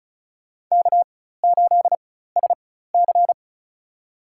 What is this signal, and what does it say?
Morse code, heard as a keyed tone, about 36 wpm, spaced as for 16 wpm, 700 Hz. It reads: K8SC
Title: USA Call Signs - 4 Characters - 4x Character Spacing - 35wpm